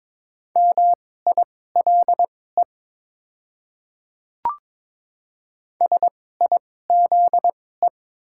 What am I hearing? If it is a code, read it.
Morse code, heard as a keyed tone, about 22 wpm, spaced as for 16 wpm, 690 Hz. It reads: MILE SIZE
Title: 2x Character Spacing - Top 500 Words - 22wpm